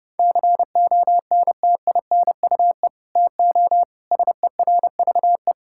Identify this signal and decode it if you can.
Morse code, heard as a keyed tone, about 30 wpm, 700 Hz. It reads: CONTINUE TO SERVE
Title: Sentences from Top 1000 Words - Review Part 6 - 30wpm